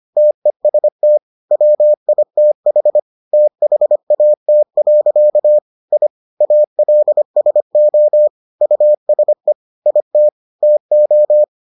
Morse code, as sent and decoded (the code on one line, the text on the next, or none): - . ... - / .-- .. - .... / - .... .- - .-.-.- / .. / .- .-.. ... --- / ..- ... / .. - / - ---
TEST WITH THAT. I ALSO USE IT TO